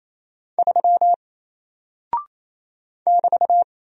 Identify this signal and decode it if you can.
Morse code, heard as a keyed tone, about 28 wpm, 700 Hz. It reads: 3 <BT>
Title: Single Character (Letter:Number:Punctuation:Prosign) Alt 28wpm